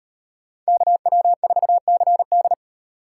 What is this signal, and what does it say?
Morse code, heard as a keyed tone, about 38 wpm, 700 Hz. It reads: KW4CD